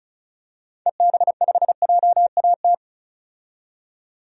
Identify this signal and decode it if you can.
Morse code, heard as a keyed tone, about 36 wpm, 700 Hz. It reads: EB5JAT
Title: DXCC-100 Call Signs - 6 Characters - 35wpm